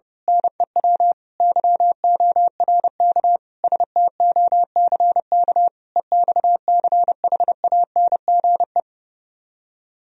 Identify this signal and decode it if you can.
Morse code, heard as a keyed tone, about 28 wpm, 700 Hz. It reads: NEW YORK STOCK EXCHANGE
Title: Speed-Racing - Sets of 5 Words 30wpm